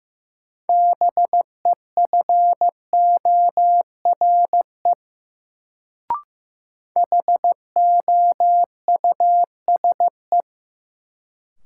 Morse code, heard as a keyed tone, about 15 wpm, 700 Hz. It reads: BEFORE HOUSE
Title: Sets of 2 Words - New Word Drill - 15wpm